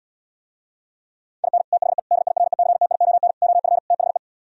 Morse code, heard as a keyed tone, about 77 wpm, 700 Hz. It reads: IN THE DIRECTION OF THE